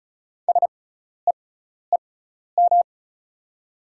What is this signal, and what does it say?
Morse code, heard as a keyed tone, about 36 wpm, spaced as for 12 wpm, 700 Hz. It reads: SEEM
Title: 6x Character Spacing - Top 300 Words - 35wpm